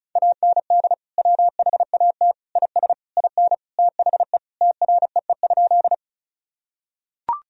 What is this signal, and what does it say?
Morse code, heard as a keyed tone, about 36 wpm, 700 Hz. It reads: AND WHAT IS IN THE TREE?
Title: Sentences from Top 1000 Words - Review Part 1 - 35wpm